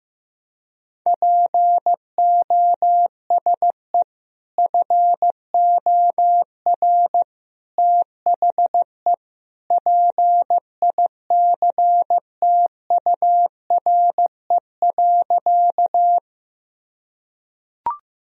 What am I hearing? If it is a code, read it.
Morse code, heard as a keyed tone, about 15 wpm, 700 Hz. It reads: POSE FOR THE PICTURE.
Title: Sentences from Top 1000 Words - Review Part 2 - 15wpm